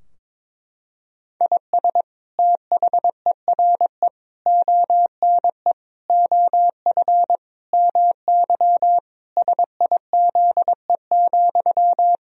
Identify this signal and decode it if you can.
Morse code, heard as a keyed tone, about 22 wpm, 700 Hz. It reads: IS THERE ONE OF MY SIZE,
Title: Sentences from Top 1000 Words - Review Part 2 - 22wpm